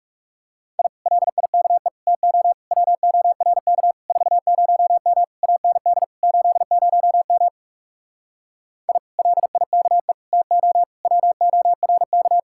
Morse code, heard as a keyed tone, about 40 wpm, 700 Hz. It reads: I LIKE TO WORK 40M AND 80M I LIKE TO WORK